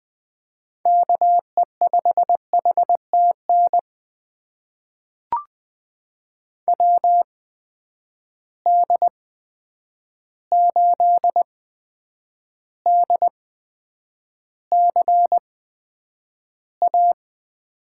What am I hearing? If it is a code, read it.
Morse code, heard as a keyed tone, about 20 wpm, spaced as for 9 wpm, 700 Hz. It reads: KE5HTN WD8DCA
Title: USA Call Signs - 6 Characters - 8x Character Spacing - 20wpm